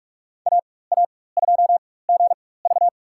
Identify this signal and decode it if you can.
Morse code, heard as a keyed tone, about 46 wpm, spaced as for 21 wpm, 690 Hz. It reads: AA2GV
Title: USA Call Signs - 4x Character Spacing - 45wpm